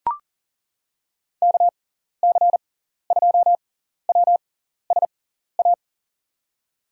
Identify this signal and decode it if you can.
Morse code, heard as a keyed tone, about 39 wpm, spaced as for 14 wpm, 700 Hz. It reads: KC2WSA
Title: USA Call Signs - 6x Character Spacing - 40wpm